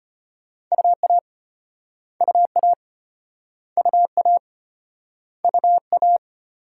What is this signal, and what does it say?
Morse code, heard as a keyed tone, about 35 wpm, 700 Hz. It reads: UA UA UA UA